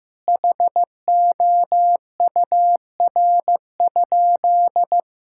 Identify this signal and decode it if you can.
Morse code, heard as a keyed tone, about 15 wpm, 700 Hz. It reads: HOUR?
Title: Sentences from Top 1000 Words - Review Part 5 - 15wpm